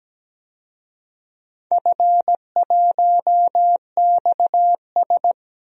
Morse code, heard as a keyed tone, about 17 wpm, 700 Hz. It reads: F1XS
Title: DXCC-100 Call Signs - 4 Characters - Repeated 2x - 17wpm